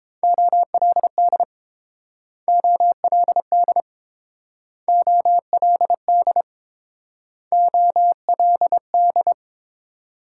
Morse code, heard as a keyed tone, about 30 wpm, 700 Hz. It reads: OLD OLD OLD OLD